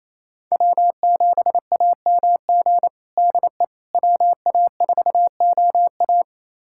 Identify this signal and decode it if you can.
Morse code, heard as a keyed tone, about 27 wpm, 700 Hz. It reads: W7AMG DE WA4OA
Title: USA Callsign DE Callsign - 5 Characters - 28wpm